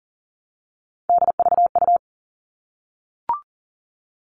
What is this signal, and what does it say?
Morse code, heard as a keyed tone, about 40 wpm, 700 Hz. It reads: DVU
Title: All Cut Numbers - 3-Digit 40wpm